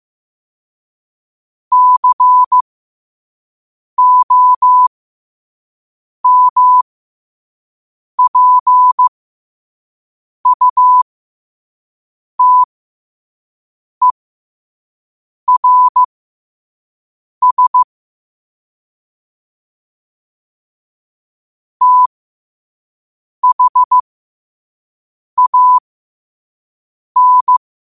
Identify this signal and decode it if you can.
Morse code, heard as a keyed tone, about 15 wpm, spaced as for 5 wpm, 1000 Hz. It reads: COMPUTERS THAN
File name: qotdfiles / 5wpm